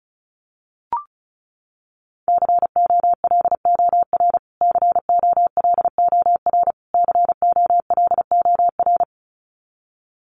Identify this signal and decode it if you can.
Morse code, heard as a keyed tone, about 34 wpm, 700 Hz. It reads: COLOR COLOR COLOR
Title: Top 500 Words Repeated 35wpm